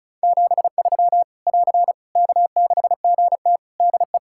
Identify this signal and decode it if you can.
Morse code, heard as a keyed tone, about 34 wpm, 700 Hz. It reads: 73 <AR> K6GT DE
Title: Practice QSO - Exchange 4 - Ending - 35wpm